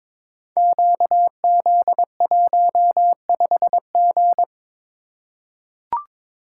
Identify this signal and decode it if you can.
Morse code, heard as a keyed tone, about 22 wpm, 690 Hz. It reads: QZ15G
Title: Code Groups - 5 Letter:Numbers - 22wpm